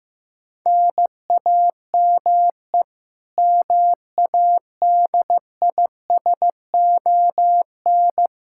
Morse code, heard as a keyed tone, about 15 wpm, 700 Hz. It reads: NAME MADISON
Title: K1USN SST QSO Practice 15wpm